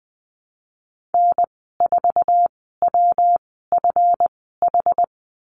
Morse code, heard as a keyed tone, about 20 wpm, spaced as for 15 wpm, 700 Hz. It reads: N4WFH
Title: 147 Lesson 24g - Letter y - Call Signs 20wpm